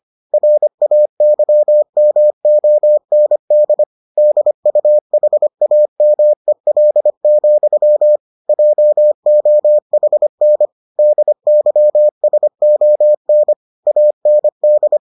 Morse code, heard as a keyed tone, about 25 wpm, 600 Hz. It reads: RAYMOND DUHAMEL, JOHN DYSON AND